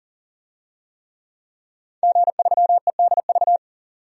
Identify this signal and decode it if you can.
Morse code, heard as a keyed tone, about 40 wpm, 700 Hz. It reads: G3EDV
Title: DXCC-100 Call Signs - 5 Characters - 40wpm